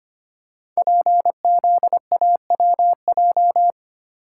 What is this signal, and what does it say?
Morse code, heard as a keyed tone, about 25 wpm, 700 Hz. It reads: PZAWJ
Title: Code Groups - 5 Letter:Numbers - 25wpm